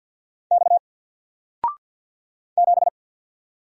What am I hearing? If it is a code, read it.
Morse code, heard as a keyed tone, about 50 wpm, 700 Hz. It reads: X 7
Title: Single Letter-Number 50wpm